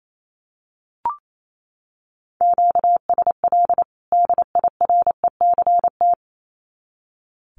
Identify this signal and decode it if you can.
Morse code, heard as a keyed tone, about 28 wpm, 700 Hz. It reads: QSL DIRECT
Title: Common QSO Phrases 28wpm